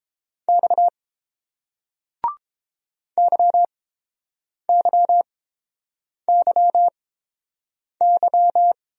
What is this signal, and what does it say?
Morse code, heard as a keyed tone, about 30 wpm, spaced as for 7 wpm, 700 Hz. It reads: X YYYY